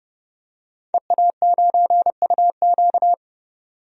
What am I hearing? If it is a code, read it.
Morse code, heard as a keyed tone, about 30 wpm, 700 Hz. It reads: EA9UQ
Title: DXCC-100 Call Signs - 5 Characters - 30wpm